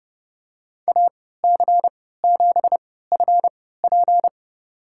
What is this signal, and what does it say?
Morse code, heard as a keyed tone, about 31 wpm, spaced as for 17 wpm, 700 Hz. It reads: AC7FP
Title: USA Call Signs - 5 Characters - 3x Character Spacing - 30wpm